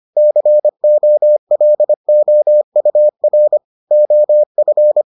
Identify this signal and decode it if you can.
Morse code, heard as a keyed tone, about 25 wpm, 600 Hz. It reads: COLOUR OF